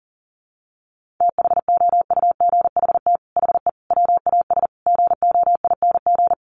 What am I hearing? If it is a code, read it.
Morse code, heard as a keyed tone, about 40 wpm, 700 Hz. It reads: THOUGHT HE WAS GOING
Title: Sets of 5 Words Encore2 40wpm